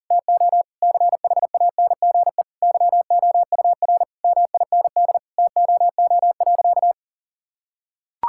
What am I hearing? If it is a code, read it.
Morse code, heard as a keyed tone, about 40 wpm, 700 Hz. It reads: TO CHANGE YOUR MIND TOO.
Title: Sentences from Top 1000 Words - Review Part 6 - 40wpm